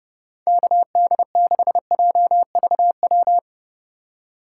Morse code, heard as a keyed tone, about 30 wpm, 700 Hz. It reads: KD6JVW
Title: USA Call Signs - 4x Character Spacing - 30wpm